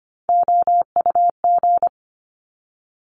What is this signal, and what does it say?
Morse code, heard as a keyed tone, about 25 wpm, 700 Hz. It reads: OUG
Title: Top Three Letter Combinations in Words 25wpm